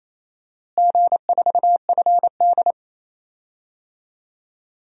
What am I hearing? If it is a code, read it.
Morse code, heard as a keyed tone, about 27 wpm, 700 Hz. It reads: G4FD
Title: DXCC-100 Call Signs - 4 Characters - Repeated 2x - 28wpm